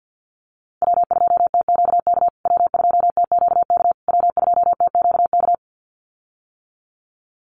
Morse code, heard as a keyed tone, about 50 wpm, 700 Hz. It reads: W2TQK W2TQK W2TQK
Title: USA Call Signs Repeated 3x 50wpm